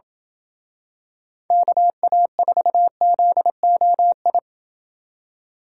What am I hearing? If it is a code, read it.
Morse code, heard as a keyed tone, about 27 wpm, 700 Hz. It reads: KA4ZOI